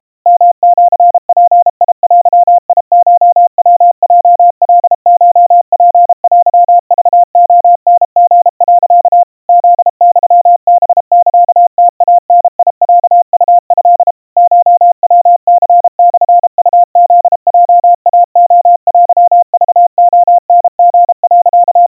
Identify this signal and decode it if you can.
Morse code (Japanese Wabun code), heard as a keyed tone, about 32 wpm, 700 Hz. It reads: ヨシヅデコヤヲカコツテクレタリ、 フユハサムイダロウト コヤニモウフヲイレテクレタリ、
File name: kawa / Morse sample / kamyu 120 cpm 700 Hz Crispy